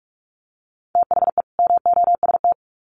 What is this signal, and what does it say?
Morse code, heard as a keyed tone, about 44 wpm, 700 Hz. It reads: THE MOST
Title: Sets of 2 Words 45wpm